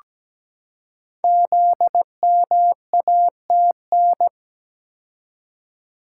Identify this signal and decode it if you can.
Morse code, heard as a keyed tone, about 17 wpm, 700 Hz. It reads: ZMATN
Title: Code Groups - 5 Letters - 17wpm